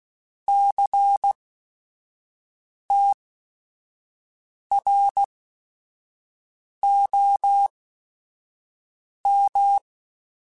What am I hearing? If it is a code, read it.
Morse code, heard as a keyed tone, about 16 wpm, spaced as for 5 wpm, 800 Hz. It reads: CTROM
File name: Technical Subjects / Technical Subjects-mp3s / XL @5